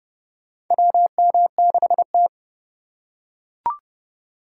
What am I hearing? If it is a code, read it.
Morse code, heard as a keyed tone, about 32 wpm, 700 Hz. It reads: WM6T
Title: USA Call Signs - 4 Characters - 4x Character Spacing - 30wpm